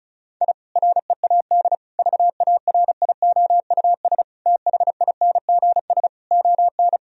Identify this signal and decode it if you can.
Morse code, heard as a keyed tone, about 36 wpm, 700 Hz. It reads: I READ VARIOUS THINGS ON
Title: Sentences from Top 1000 Words - Review Part 4 - 35wpm